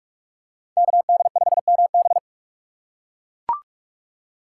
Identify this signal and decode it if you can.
Morse code, heard as a keyed tone, about 42 wpm, 700 Hz. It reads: KD5MB
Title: USA Call Signs - 5 Characters - 2x Character Spacing - 45wpm